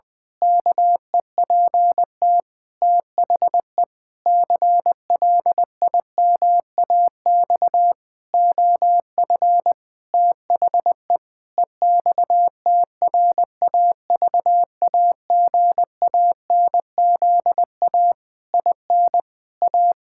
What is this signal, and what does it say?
Morse code, heard as a keyed tone, about 20 wpm, 700 Hz. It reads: KEPT THE CLIMAX OF THE EXTRAVAGANZA IN A
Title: Pangram Challenge v2 - 20wpm